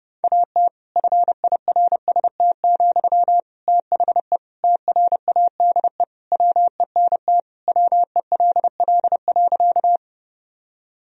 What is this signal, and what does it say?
Morse code, heard as a keyed tone, about 30 wpm, 700 Hz. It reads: AT FIRST, THE TRADE WENT WELL.
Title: Sentences from Top 1000 Words - Review Part 4 - 30wpm